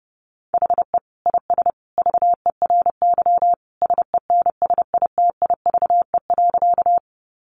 Morse code, heard as a keyed tone, about 30 wpm, 700 Hz. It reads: HE IS VERY SENSITIVE.
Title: Sentences from Top 1400 Words 30wpm